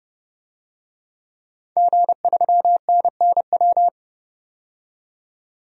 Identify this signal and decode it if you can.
Morse code, heard as a keyed tone, about 29 wpm, 700 Hz. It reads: G3NNW
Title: DXCC-100 Call Signs - 5 Characters - 30wpm